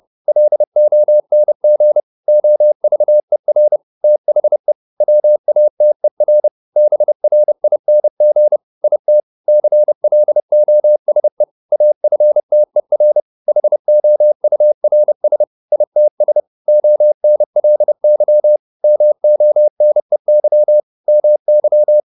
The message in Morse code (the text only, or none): LONG OVER THE WATER BRING IT CLOSE AFTER HOURS ITS ONLY MONEY MY